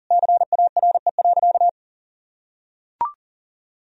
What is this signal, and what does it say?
Morse code, heard as a keyed tone, about 40 wpm, 700 Hz. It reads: CARE.
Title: Sentences from Top 1000 Words - Review Part 1 - 40wpm